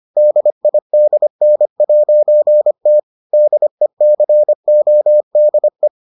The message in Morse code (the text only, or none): DIDN'T DECODE